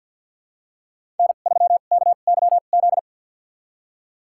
Morse code, heard as a keyed tone, about 47 wpm, spaced as for 35 wpm, 700 Hz. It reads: N3KYZ
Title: USA Call Signs - 2x Character Spacing - 50wpm